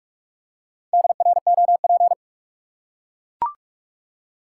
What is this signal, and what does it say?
Morse code, heard as a keyed tone, about 42 wpm, 700 Hz. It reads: DROP
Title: Four Letter Words - 45wpm